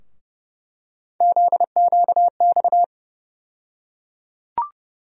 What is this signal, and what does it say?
Morse code, heard as a keyed tone, about 31 wpm, 700 Hz. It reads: ZQX
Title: USA Call Sign Postfixes - 3 Letters 30wpm